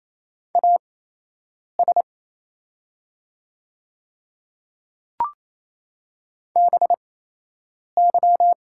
Morse code, heard as a keyed tone, about 28 wpm, spaced as for 8 wpm, 700 Hz. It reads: AS BY
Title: 8x Character Spacing - Top 100 Words - 28wpm